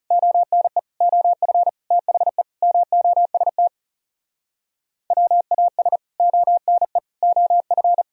Morse code, heard as a keyed tone, about 40 wpm, 700 Hz. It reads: ONE OF THE MOST WAS ONE OF